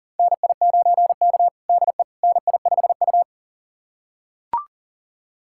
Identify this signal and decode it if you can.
Morse code, heard as a keyed tone, about 40 wpm, 700 Hz. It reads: NI9K DE NI5U
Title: USA Callsign DE Callsign - 4 Characters - 40wpm